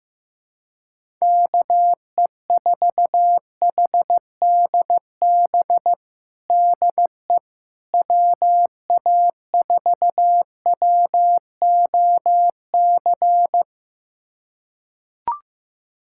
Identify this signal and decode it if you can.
Morse code, heard as a keyed tone, about 15 wpm, 700 Hz. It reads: KE4HDB DE WA4WOC
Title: USA Callsign DE Callsign - 6 Characters - 15wpm